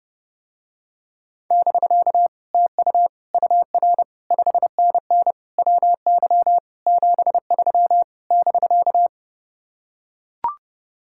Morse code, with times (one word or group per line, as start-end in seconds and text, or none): 1.5-2.3 <BK>
2.5-3.1 TU
3.3-4.0 UR
4.3-5.3 5NN
5.6-6.6 WY
6.9-8.0 73
8.3-9.1 <BK>